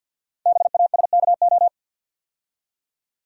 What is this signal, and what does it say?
Morse code, heard as a keyed tone, about 49 wpm, 700 Hz. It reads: BASKO
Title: Code Groups - 5 Letters - 50wpm